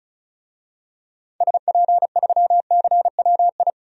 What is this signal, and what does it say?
Morse code, heard as a keyed tone, about 34 wpm, 700 Hz. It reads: SP3CWI
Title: DXCC-100 Call Signs - 6 Characters - 35wpm